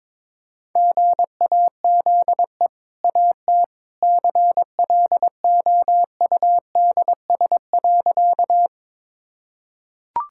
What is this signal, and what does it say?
Morse code, heard as a keyed tone, about 22 wpm, 700 Hz. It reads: GAZE AT CLOUDS.
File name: practice / 22wpm / Easy Sentences with Hints - 22wpm